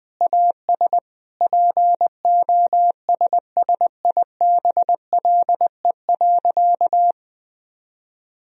Morse code, heard as a keyed tone, about 20 wpm, 700 Hz. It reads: AS POSSIBLE.
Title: Sentences from Top 1000 Words - Review Part 4 - 20wpm